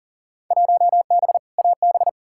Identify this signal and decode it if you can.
Morse code, heard as a keyed tone, about 39 wpm, 700 Hz. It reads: 1B AB
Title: Field Day Exchanges - 40wpm